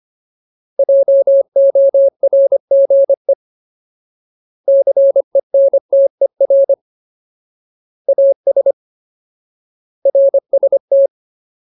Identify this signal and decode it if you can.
Morse code, heard as a keyed tone, about 25 wpm, 550 Hz. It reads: JORGE CENTER AS RST
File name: Ultimate Word List / 25WPM Ultimate